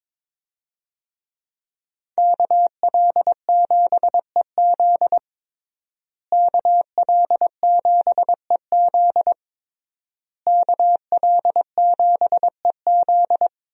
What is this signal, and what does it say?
Morse code, heard as a keyed tone, about 22 wpm, 700 Hz. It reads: KL7EZ KL7EZ KL7EZ
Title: DXCC-100 Call Signs - 5 Characters - Repeated 3x - 22wpm